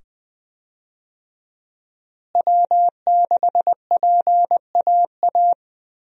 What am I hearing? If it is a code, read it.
Morse code, heard as a keyed tone, about 20 wpm, 700 Hz. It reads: W6PAA
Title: USA Call Signs - 5 Characters - Repeated 2x - 20wpm